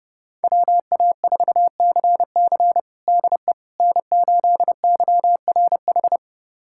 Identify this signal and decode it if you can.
Morse code, heard as a keyed tone, about 32 wpm, 700 Hz. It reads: WA4CC DE N8YRH
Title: USA Callsign DE Callsign - 5 Characters - 30wpm